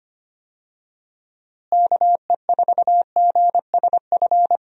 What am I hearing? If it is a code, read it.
Morse code, heard as a keyed tone, about 25 wpm, 700 Hz. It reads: KE4GSF
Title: USA Call Signs - 6 Characters - Repeated 2x - 25wpm